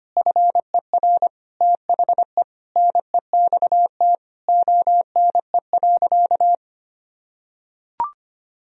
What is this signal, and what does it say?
Morse code, heard as a keyed tone, about 25 wpm, 700 Hz. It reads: FER THE NEXT ONE.
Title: Sentences from Top 400 Words and Abbreviations 25wpm